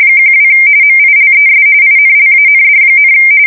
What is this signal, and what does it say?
RTTY, about 45.45 baud. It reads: 12 MILES INLAND,
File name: fsk170 / USS-New-Jersey